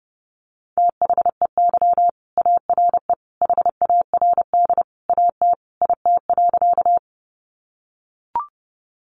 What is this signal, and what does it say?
Morse code, heard as a keyed tone, about 30 wpm, 700 Hz. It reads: THEY ARE HARD AT IT.
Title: Sentences from Top 300 Words 30wpm